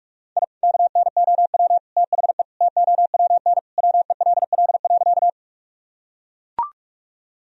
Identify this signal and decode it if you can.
Morse code, heard as a keyed tone, about 43 wpm, 700 Hz. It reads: I KNOW THE TOWN WELL.
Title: Sentences from Top 1000 Words - Review Part 2 - 45wpm